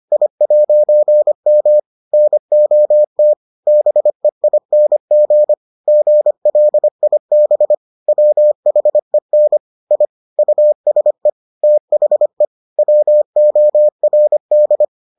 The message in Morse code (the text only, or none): I'M NOT BEING GLIB WHEN I USE THE WORD